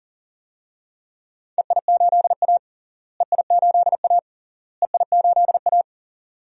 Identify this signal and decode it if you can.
Morse code, heard as a keyed tone, about 40 wpm, 700 Hz. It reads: EI8A EI8A EI8A
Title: DXCC-100 Call Signs - 4 Characters - Repeated 3x - 40wpm